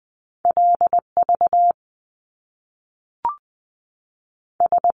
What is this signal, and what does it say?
Morse code, heard as a keyed tone, about 20 wpm, 700 Hz. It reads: L V S
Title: Top Two Letter Combinations in Words 20wpm